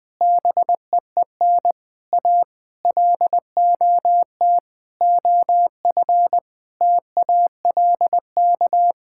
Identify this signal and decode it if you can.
Morse code, heard as a keyed tone, about 21 wpm, 700 Hz. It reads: BEEN A LOT OF TALK
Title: Sets of 8 Words - 20wpm